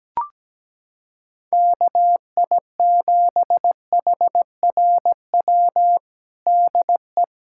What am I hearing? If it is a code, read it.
Morse code, heard as a keyed tone, about 17 wpm, 700 Hz. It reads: KI7HRW DE
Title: USA Callsign DE Callsign - 6 Characters - 17wpm